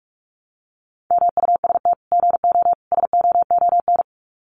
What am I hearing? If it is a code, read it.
Morse code, heard as a keyed tone, about 44 wpm, 700 Hz. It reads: MUST GO SOON
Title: Common QSO Phrases 45wpm